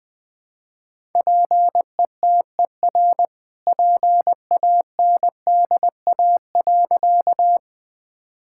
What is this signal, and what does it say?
Morse code, heard as a keyed tone, about 21 wpm, 700 Hz. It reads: PETER PANDA.
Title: All-Age Appropriate Jokes - 20wpm